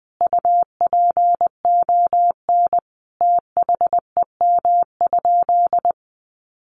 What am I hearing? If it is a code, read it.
Morse code, heard as a keyed tone, about 20 wpm, 700 Hz. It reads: UPON THEM?
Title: Sentences from Top 1300 Words 20wpm